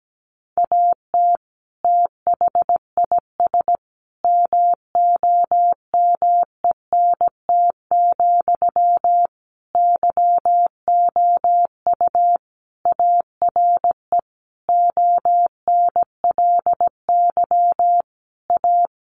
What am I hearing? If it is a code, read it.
Morse code, heard as a keyed tone, about 17 wpm, 700 Hz. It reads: AT THIS MOMENT, YOU ARE ONLY A
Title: Sentences from Top 1000 Words 17wpm